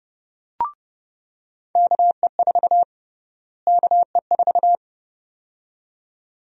Morse code, heard as a keyed tone, about 30 wpm, 700 Hz. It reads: KE4 KE4
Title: USA Call Sign Prefixes - Repeated 2x - 30wpm